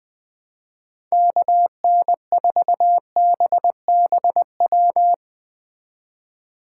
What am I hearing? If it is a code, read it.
Morse code, heard as a keyed tone, about 20 wpm, 700 Hz. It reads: KN4BBW